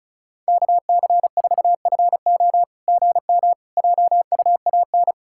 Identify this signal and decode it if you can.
Morse code, heard as a keyed tone, about 36 wpm, 700 Hz. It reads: KC4FO GM JUAN
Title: SOTA QSO Practice 35wpm